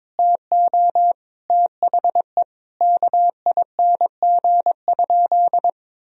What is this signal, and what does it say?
Morse code, heard as a keyed tone, about 22 wpm, 700 Hz. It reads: TO THE KING?
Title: Sentences from Top 1000 Words - Review Part 2 - 22wpm